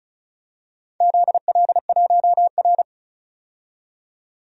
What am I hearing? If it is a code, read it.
Morse code, heard as a keyed tone, about 34 wpm, 700 Hz. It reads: ZL1R
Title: DXCC-100 Call Signs - 4 Characters - 35wpm